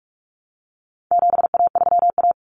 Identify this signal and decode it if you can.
Morse code, heard as a keyed tone, about 44 wpm, 700 Hz. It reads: 7A3A
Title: Common Cut Numbers - 4-Digit 45wpm